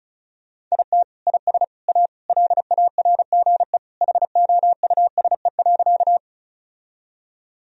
Morse code, heard as a keyed tone, about 36 wpm, 700 Hz. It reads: IT IS A LARGE HOUSE.
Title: Sentences from Top 1000 Words - Review Part 1 - 35wpm